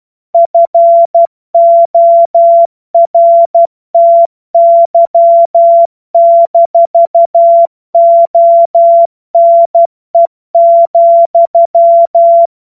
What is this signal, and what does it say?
Morse code, heard as a keyed tone, about 12 wpm, 680 Hz. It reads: FORTY-ONE,
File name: Morsekurs / Witze